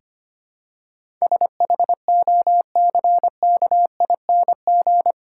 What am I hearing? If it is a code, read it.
Morse code, heard as a keyed tone, about 25 wpm, 700 Hz. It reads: SHOCKING
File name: 8 Letter Words - First 3 Letters Given - 25wpm